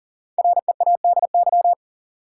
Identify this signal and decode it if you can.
Morse code, heard as a keyed tone, about 40 wpm, 700 Hz. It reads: READY
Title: Five Letter Words - 40wpm